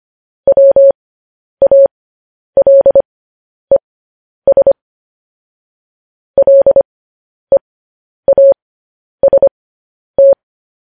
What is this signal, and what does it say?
Morse code, heard as a keyed tone, about 25 wpm, spaced as for 10 wpm, 550 Hz. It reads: WALES LEAST
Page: news-headlines-in-morse-code-at-10-wpm